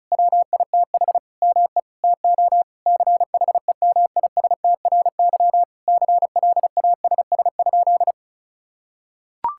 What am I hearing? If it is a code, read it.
Morse code, heard as a keyed tone, about 36 wpm, 700 Hz. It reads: WITH ME TO CHEMISTRY CLASS?
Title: Sentences from Top 1000 Words - Review Part 6 - 35wpm